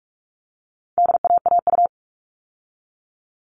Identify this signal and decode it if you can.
Morse code, heard as a keyed tone, about 44 wpm, 700 Hz. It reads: DAAU
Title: All Cut Numbers - 4-Digit 45wpm